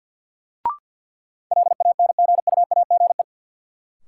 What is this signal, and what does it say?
Morse code, heard as a keyed tone, about 52 wpm, 700 Hz. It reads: LANGUAGE
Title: Sentences from Top 500 Words - New Word Drill - 50wpm